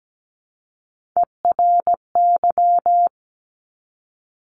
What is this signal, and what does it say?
Morse code, heard as a keyed tone, about 17 wpm, 700 Hz. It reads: ERY
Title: Top Three Letter Combinations in Words 17wpm